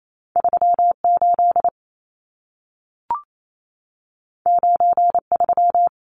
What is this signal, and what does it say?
Morse code, heard as a keyed tone, about 28 wpm, 700 Hz. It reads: 38 93
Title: Numbers - 2-Digit 28wpm